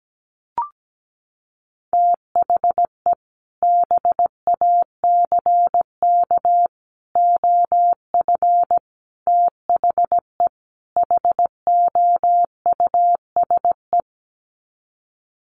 Morse code, heard as a keyed tone, about 17 wpm, 700 Hz. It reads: THE BACK OF THE HOUSE
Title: Sets of 5 Words Encore2 17wpm